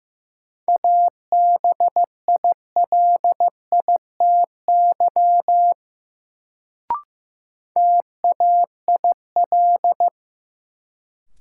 Morse code, heard as a keyed tone, about 15 wpm, 700 Hz. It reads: ABILITY TAIL
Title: Sentences from Top 1000 Words - Review Part 3 - New Word Drill - 15wpm